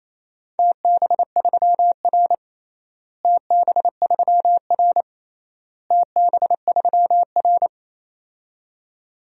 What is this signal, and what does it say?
Morse code, heard as a keyed tone, about 28 wpm, 700 Hz. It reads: TB3R TB3R TB3R
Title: DXCC-100 Call Signs - 4 Characters - Repeated 3x - 28wpm